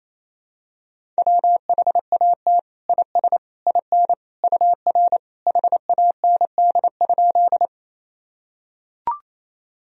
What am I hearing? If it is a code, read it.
Morse code, heard as a keyed tone, about 28 wpm, 700 Hz. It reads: WHAT IS IN UR HAND?
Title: Sentences from Top 200 Words and Abbreviations 28wpm